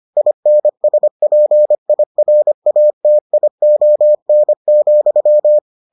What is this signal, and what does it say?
Morse code, heard as a keyed tone, about 25 wpm, 600 Hz. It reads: INSPIRATION,